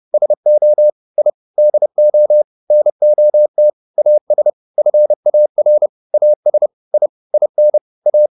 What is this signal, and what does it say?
Morse code, heard as a keyed tone, about 30 wpm, 600 Hz. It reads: SO I DO NOT AS FAR AS I IN A